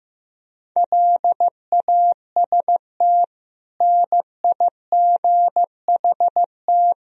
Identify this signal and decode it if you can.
Morse code, heard as a keyed tone, about 16 wpm, 700 Hz. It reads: LAST NIGHT
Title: Sentences from Top 1000 Words - Review Part 4 - 15wpm